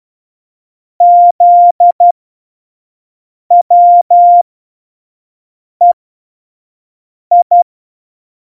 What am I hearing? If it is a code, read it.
Morse code, heard as a keyed tone, about 12 wpm, spaced as for 5 wpm, 700 Hz. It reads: ZWEI
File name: Morsekurs / Witze / Witz 03 April 2018, 25 F70